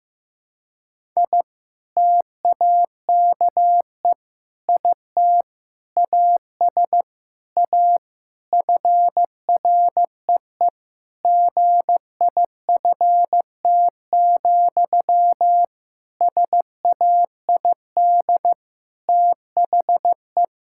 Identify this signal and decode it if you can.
Morse code, heard as a keyed tone, about 16 wpm, 700 Hz. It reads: I TAKE IT AS A FREE GIFT, SAID THE
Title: Sentences from Top 1000 Words - Review Part 6 - 15wpm